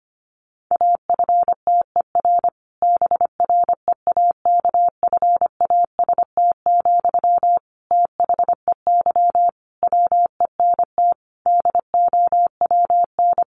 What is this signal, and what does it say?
Morse code, heard as a keyed tone, about 25 wpm, 700 Hz. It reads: AFTER BREAKFAST, THEY WENT DOWN
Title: Sentences from Top 1500 Words 25wpm